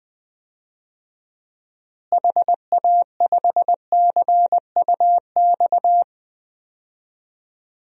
Morse code, heard as a keyed tone, about 20 wpm, 700 Hz. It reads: HA5CUX
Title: DXCC-100 Call Signs - 6 Characters - Repeated 2x - 20wpm